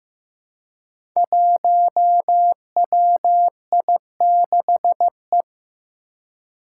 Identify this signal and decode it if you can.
Morse code, heard as a keyed tone, about 15 wpm, 700 Hz. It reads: 1WI6E
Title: Five Challenging Letter-Number Combinations 15wpm